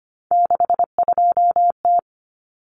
Morse code, heard as a keyed tone, about 25 wpm, 700 Hz. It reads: 62T
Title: Common Cut Numbers - 3-Digit 25wpm